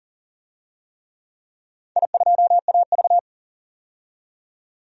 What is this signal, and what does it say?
Morse code, heard as a keyed tone, about 40 wpm, 700 Hz. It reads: I2AV
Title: DXCC-100 Call Signs - 4 Characters - 40wpm